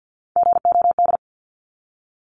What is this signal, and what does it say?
Morse code, heard as a keyed tone, about 48 wpm, 700 Hz. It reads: GOD
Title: Top 1751-2000 Words 50wpm